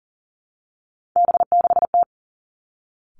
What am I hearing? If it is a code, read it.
Morse code, heard as a keyed tone, about 38 wpm, 700 Hz. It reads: B6T
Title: Three Challenging Letter-Number Combinations 40wpm